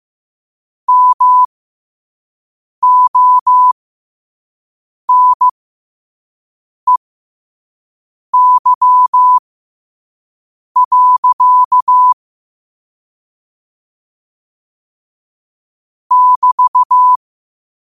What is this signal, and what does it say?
Morse code, heard as a keyed tone, about 15 wpm, spaced as for 5 wpm, 1000 Hz. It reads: MONEY. <BT>